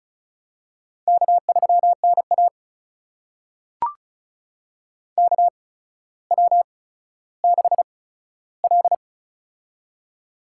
Morse code, heard as a keyed tone, about 36 wpm, spaced as for 15 wpm, 700 Hz. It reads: K3NA KW6L
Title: USA Call Signs - 4 Characters - 8x Character Spacing - 35wpm